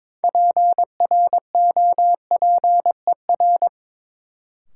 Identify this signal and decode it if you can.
Morse code, heard as a keyed tone, about 22 wpm, 700 Hz. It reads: PROPER
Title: Sentences from Top 800 Words - New Word Drill - 22wpm